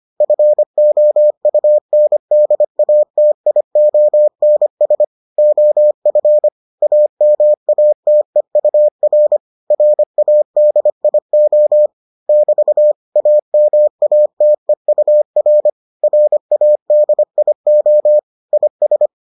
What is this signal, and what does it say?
Morse code, heard as a keyed tone, about 25 wpm, 600 Hz. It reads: FOUNDATIONS OF AMATEUR RADIO <BT> AMATEUR RADIO IS